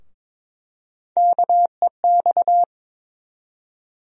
Morse code, heard as a keyed tone, about 22 wpm, 700 Hz. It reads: KEX
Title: USA Call Sign Postfixes - 3 Letters 22wpm